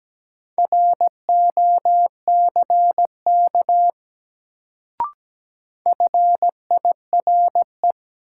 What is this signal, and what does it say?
Morse code, heard as a keyed tone, about 17 wpm, 700 Hz. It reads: ROCK FIRE
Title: Sentences from Top 400 Words - New Word Drill - 17wpm